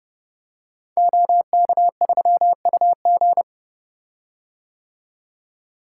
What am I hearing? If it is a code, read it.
Morse code, heard as a keyed tone, about 28 wpm, 700 Hz. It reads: OK3UG
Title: Code Groups - 5 Letter:Numbers - 30wpm